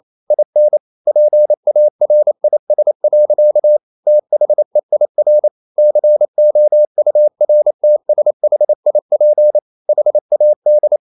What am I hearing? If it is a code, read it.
Morse code, heard as a keyed tone, about 28 wpm, 600 Hz. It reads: IN PARIS. THEIR COURTSHIP HAD
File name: Never 0027wpm